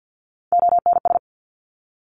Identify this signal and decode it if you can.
Morse code, heard as a keyed tone, about 50 wpm, 700 Hz. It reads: ONS
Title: Top Three Letter Combinations in Words 50wpm